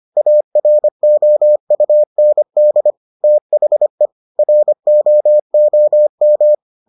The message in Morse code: AROUND THE ROOM

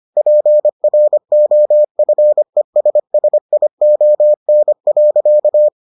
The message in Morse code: PROFESSION.

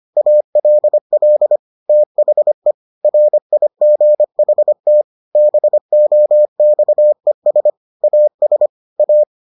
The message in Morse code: ALL THE RIGHT BOXES AS A

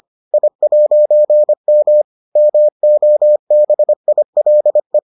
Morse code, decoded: I'M MOBILE

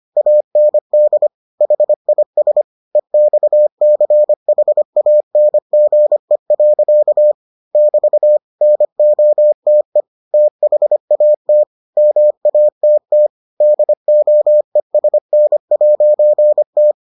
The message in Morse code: AND HIS EXCHANGE. <BT> NOTE THAT MATT DOESN'T

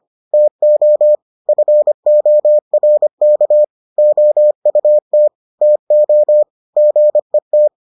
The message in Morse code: TO FORK OUT TO GET

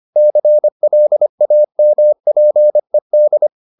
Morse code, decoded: CLAMPED

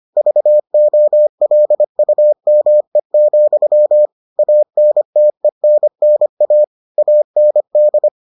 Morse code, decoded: VOLUME, ANTENNA AND

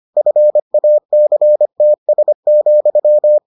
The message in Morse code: FACTS,

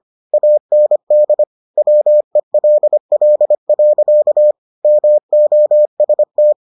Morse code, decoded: AND WELL. MOST

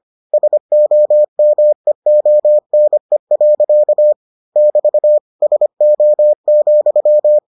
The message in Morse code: SOMEONE. <BT> SO,

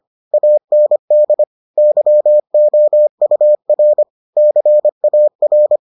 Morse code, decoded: AND YOUR CAR

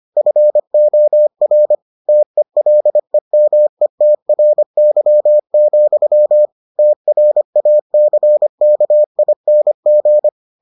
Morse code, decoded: FOR TELEMETRY, TRACKING

